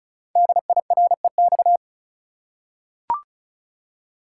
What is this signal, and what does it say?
Morse code, heard as a keyed tone, about 36 wpm, 690 Hz. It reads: DIREX
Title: Code Groups - 5 Letters - 35wpm